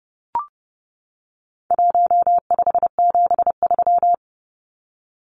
Morse code, heard as a keyed tone, about 30 wpm, 700 Hz. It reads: 1573
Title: Numbers - Contest Style - 4-Digit 30wpm